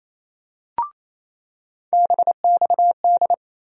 Morse code, heard as a keyed tone, about 28 wpm, 700 Hz. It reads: BXD
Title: USA Call Sign Postfixes - 3 Letters 28wpm